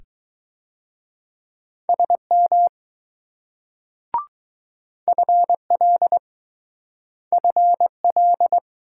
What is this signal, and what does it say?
Morse code, heard as a keyed tone, about 20 wpm, 700 Hz. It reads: SM FL FL